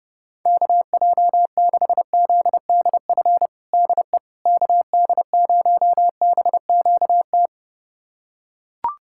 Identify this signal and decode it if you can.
Morse code, heard as a keyed tone, about 30 wpm, 700 Hz. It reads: KJ6ZDF DE KD0BQT
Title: USA Callsign DE Callsign - 6 Characters - 30wpm